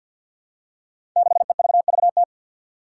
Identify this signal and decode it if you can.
Morse code, heard as a keyed tone, about 50 wpm, 700 Hz. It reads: 6EVVT
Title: Five Challenging Letter-Number Combinations 50wpm